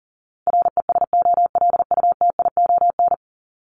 Morse code, heard as a keyed tone, about 40 wpm, 700 Hz. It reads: RESOLUTION
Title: Top 1501-1750 Words 40wpm